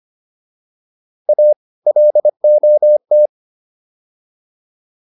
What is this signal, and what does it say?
Morse code, heard as a keyed tone, about 25 wpm, 600 Hz. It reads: A LOT